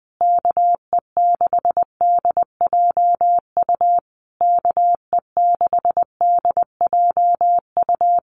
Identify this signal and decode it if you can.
Morse code, heard as a keyed tone, about 20 wpm, 700 Hz. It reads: KE6DJU KE6DJU
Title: Common QSO Phrases Encore 20wpm